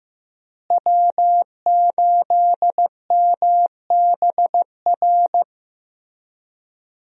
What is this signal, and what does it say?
Morse code, heard as a keyed tone, about 15 wpm, 700 Hz. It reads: W8MBR
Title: USA Call Signs - 5 Characters - Repeated 2x - 15wpm